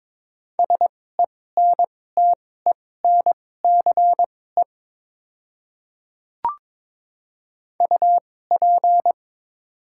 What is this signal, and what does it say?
Morse code, heard as a keyed tone, about 22 wpm, spaced as for 16 wpm, 700 Hz. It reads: SENTENCE UP
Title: 2x Character Spacing - Top 200 Words - 22wpm